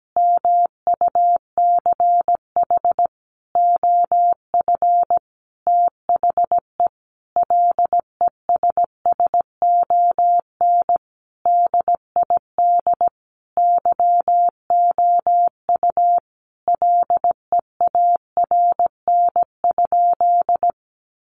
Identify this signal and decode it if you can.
Morse code, heard as a keyed tone, about 17 wpm, 700 Hz. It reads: MUCH OF THE LESSON DID YOU LEARN?
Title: Sentences from Top 1200 Words 17wpm